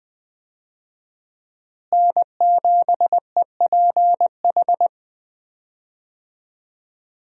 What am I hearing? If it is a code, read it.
Morse code, heard as a keyed tone, about 20 wpm, 700 Hz. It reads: N7EPH
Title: USA Call Signs - 5 Characters - Repeated 2x - 20wpm